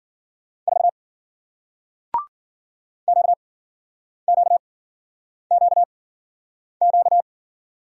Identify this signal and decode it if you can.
Morse code, heard as a keyed tone, about 53 wpm, spaced as for 9 wpm, 700 Hz. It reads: 4 QQQQ